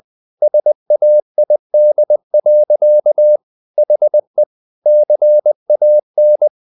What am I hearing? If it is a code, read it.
Morse code, heard as a keyed tone, about 20 wpm, 600 Hz. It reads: SAID. HE CAN
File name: the old man and the sea part 2 20wpm